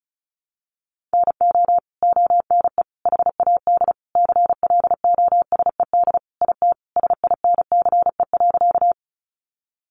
Morse code, heard as a keyed tone, about 34 wpm, 700 Hz. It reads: NO ONE HAD CLOSED IT SINCE.
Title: Sentences from Top 1400 Words 35wpm